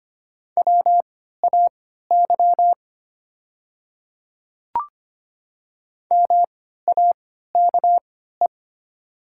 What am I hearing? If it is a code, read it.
Morse code, heard as a keyed tone, about 25 wpm, spaced as for 14 wpm, 700 Hz. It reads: WAY MAKE
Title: 3x Character Spacing - Top 200 Words - 25wpm